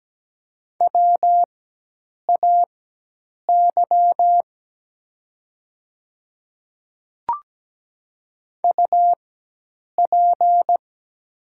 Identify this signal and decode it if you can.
Morse code, heard as a keyed tone, about 17 wpm, spaced as for 8 wpm, 700 Hz. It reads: WAY UP
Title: 4x Character Spacing - Top 100 Words - 17wpm